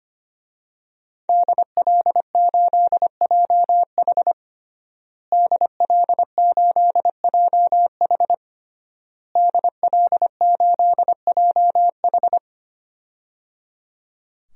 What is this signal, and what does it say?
Morse code, heard as a keyed tone, about 25 wpm, 700 Hz. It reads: DL8JH DL8JH DL8JH
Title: DXCC-100 Call Signs - 5 Characters - Repeated 3x - 25wpm